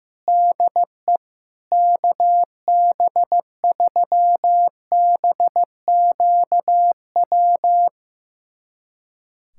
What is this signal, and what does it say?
Morse code, heard as a keyed tone, about 15 wpm, 700 Hz. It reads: DE KB3BQW
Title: USA Callsign DE Callsign - 6 Characters - 15wpm